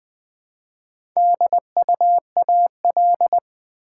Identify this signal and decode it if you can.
Morse code, heard as a keyed tone, about 20 wpm, 700 Hz. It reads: DUAL
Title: Build A Word - 1 Letter At A Time - 4-Letter Words - 20wpm